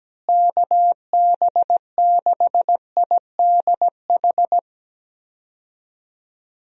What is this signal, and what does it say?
Morse code, heard as a keyed tone, about 17 wpm, 700 Hz. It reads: KB6IDH
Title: USA Call Signs - 6 Characters - Repeated 2x - 17wpm